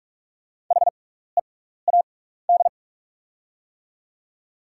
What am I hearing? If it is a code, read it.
Morse code, heard as a keyed tone, about 46 wpm, spaced as for 16 wpm, 690 Hz. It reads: HEAD
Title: 6x Character Spacing - Top 300 Words - 45wpm